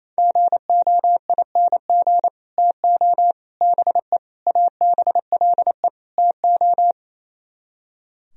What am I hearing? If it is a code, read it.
Morse code, heard as a keyed tone, about 32 wpm, 700 Hz. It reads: GOING TO BE ABLE TO
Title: Speed-Racing - Sets of 5 Words 28wpm